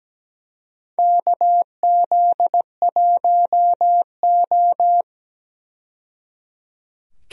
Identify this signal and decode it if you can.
Morse code, heard as a keyed tone, about 17 wpm, 700 Hz. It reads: KZ1O
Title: USA Call Signs - 4 Characters - 17wpm